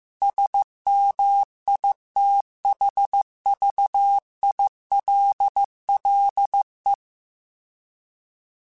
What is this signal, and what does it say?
Morse code, heard as a keyed tone, about 15 wpm, 800 Hz. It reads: SMITHVILLE